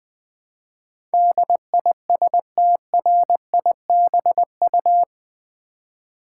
Morse code, heard as a keyed tone, about 20 wpm, 700 Hz. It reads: DISTRIBU
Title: Build A Word - 4 Letters At A Time - 10-Letter Words - 20wpm